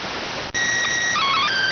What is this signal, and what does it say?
SSTV (slow-scan television), PD 290.